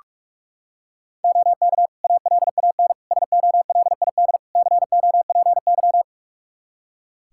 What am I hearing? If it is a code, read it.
Morse code, heard as a keyed tone, about 46 wpm, 700 Hz. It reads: OK ALAN SOLID COPY
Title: Practice QSO - Exchange 3 - More Info - 45wpm